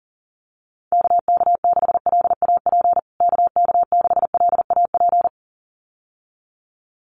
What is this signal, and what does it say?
Morse code, heard as a keyed tone, about 40 wpm, 700 Hz. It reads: KK6LAP KK6LAP
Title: USA Call Signs Repeated 2x 40wpm